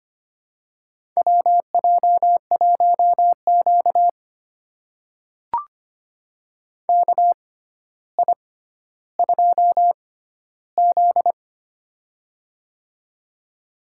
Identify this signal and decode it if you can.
Morse code, heard as a keyed tone, about 25 wpm, spaced as for 13 wpm, 700 Hz. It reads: WJ1Q KI2Z
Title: USA Call Signs - 4 Characters - 6x Character Spacing - 25wpm